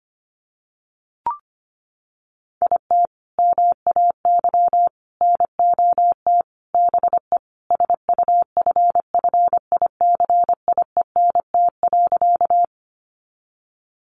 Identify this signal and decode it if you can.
Morse code, heard as a keyed tone, about 25 wpm, 700 Hz. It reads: IT MAY NOT BE SUFFICIENT.
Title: Sentences from Top 1400 Words 25wpm